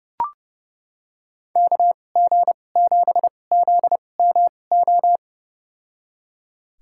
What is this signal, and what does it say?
Morse code, heard as a keyed tone, about 30 wpm, spaced as for 22 wpm, 700 Hz. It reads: KG7ZMO
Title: USA Call Signs - 6 Characters - 2x Character Spacing - 30wpm